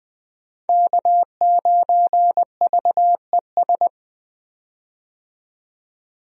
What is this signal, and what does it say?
Morse code, heard as a keyed tone, about 20 wpm, 700 Hz. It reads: K9VES
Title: USA Call Signs - 5 Characters - 20wpm